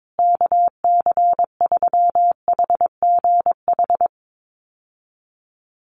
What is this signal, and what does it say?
Morse code, heard as a keyed tone, about 22 wpm, 700 Hz. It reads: KC3HGH
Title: USA Call Signs Repeated 2x 22wpm